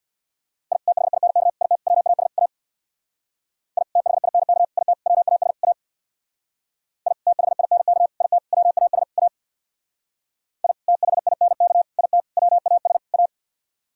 Morse code, heard as a keyed tone, about 61 wpm, 700 Hz. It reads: I THINK IT WAS A I THINK IT WAS A I THINK IT WAS A I THINK IT WAS A
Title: Speed-Racing - Sets of 5 Words 50wpm